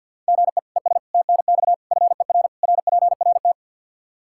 Morse code, heard as a keyed tone, about 51 wpm, 700 Hz. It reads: GE ES TNX FER RPRT
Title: Practice QSO - Exchange 1 - Beginning - 50wpm